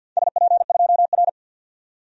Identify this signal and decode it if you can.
Morse code, heard as a keyed tone, about 50 wpm, 700 Hz. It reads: SP2R
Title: DXCC-100 Call Signs - 4 Characters - 50wpm